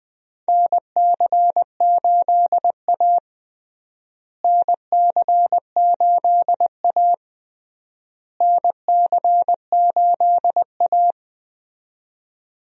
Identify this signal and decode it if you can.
Morse code, heard as a keyed tone, about 20 wpm, 700 Hz. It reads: NC8A NC8A NC8A
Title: USA Call Signs - 4 Characters - Repeated 3x - 20wpm